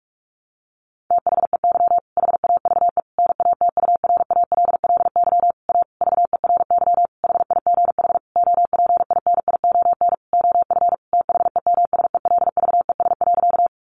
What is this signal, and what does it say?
Morse code, heard as a keyed tone, about 44 wpm, 700 Hz. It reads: THEY HAVE NATURALLY A VERY HIGH OPINION OF THEMSELVES.
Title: Sentences from Top 1300 Words 45wpm